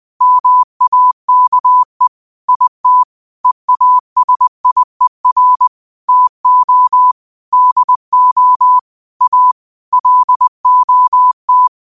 Morse code, heard as a keyed tone, about 20 wpm, 1000 Hz. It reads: MAKE IT EASIER TO DO A LOT